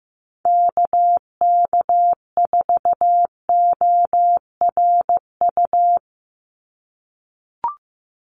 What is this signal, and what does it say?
Morse code, heard as a keyed tone, about 15 wpm, 700 Hz. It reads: KK4ORU
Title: USA Call Signs 15wpm